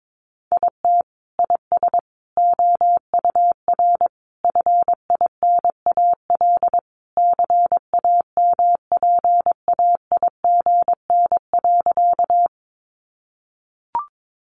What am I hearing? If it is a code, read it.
Morse code, heard as a keyed tone, about 22 wpm, 700 Hz. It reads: IT IS OUR FINAL CAMPAIGN.
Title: Sentences from Top 1200 Words 22wpm